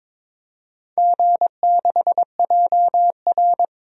Morse code, heard as a keyed tone, about 22 wpm, 700 Hz. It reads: G6JR
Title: DXCC-100 Call Signs - 4 Characters - 22wpm